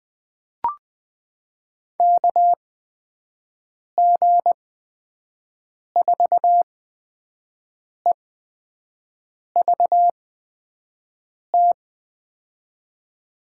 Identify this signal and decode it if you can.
Morse code, heard as a keyed tone, about 20 wpm, spaced as for 5 wpm, 700 Hz. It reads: KG4EVT
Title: USA Call Signs - 6 Characters - 8x Character Spacing - 20wpm